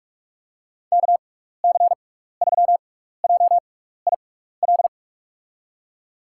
Morse code, heard as a keyed tone, about 47 wpm, spaced as for 16 wpm, 690 Hz. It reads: KC3JIL
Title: USA Call Signs - 6x Character Spacing - 45wpm